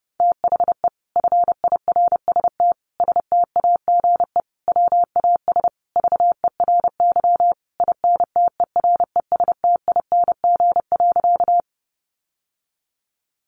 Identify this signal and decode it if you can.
Morse code, heard as a keyed tone, about 28 wpm, 700 Hz. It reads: THE FIRST STAGE WAS VERY INTERESTING.
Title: Sentences from Top 1200 Words 30wpm